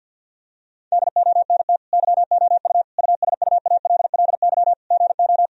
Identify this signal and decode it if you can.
Morse code, heard as a keyed tone, about 50 wpm, 700 Hz. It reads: DONT YOU USUALLY GO
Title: Sentences from Top 1000 Words - Review Part 3 - 50wpm